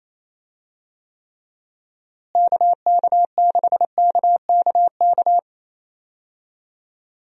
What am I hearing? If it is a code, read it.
Morse code, heard as a keyed tone, about 28 wpm, 700 Hz. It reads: KK6KKK
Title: USA Call Signs - 6 Characters - Repeated 3x - 28wpm